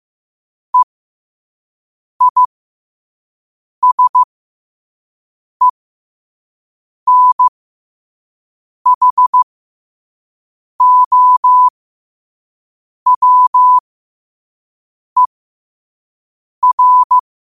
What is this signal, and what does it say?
Morse code, heard as a keyed tone, about 15 wpm, spaced as for 5 wpm, 1000 Hz. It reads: EISENHOWER
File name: qotdfiles / 5wpm